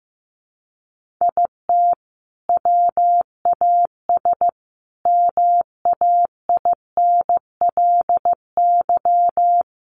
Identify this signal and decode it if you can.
Morse code, heard as a keyed tone, about 15 wpm, 700 Hz. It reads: IT WAS MAINLY
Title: Sentences from Top 1400 Words 15wpm